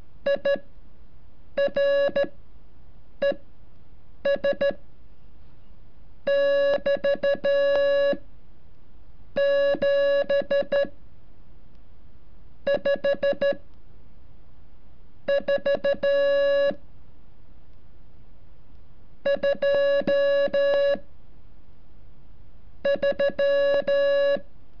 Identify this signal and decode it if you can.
Morse code, heard as a keyed tone, about 13 wpm, spaced as for 7 wpm, 560 Hz. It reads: IRES <BT> 7 5 4 2 3
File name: curso / ejercicio 33